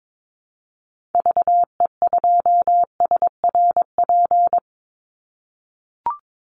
Morse code, heard as a keyed tone, about 22 wpm, 700 Hz. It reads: VE2SRP
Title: Call Signs - 22wpm